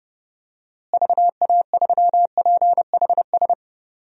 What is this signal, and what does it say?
Morse code, heard as a keyed tone, about 30 wpm, 700 Hz. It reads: VA3PHS
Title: DXCC-100 Call Signs - 6 Characters - Repeated 3x - 30wpm